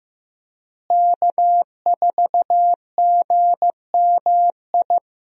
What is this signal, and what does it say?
Morse code, heard as a keyed tone, about 15 wpm, 700 Hz. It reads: K4GMI
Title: USA Call Signs - 5 Characters - Repeated 2x - 15wpm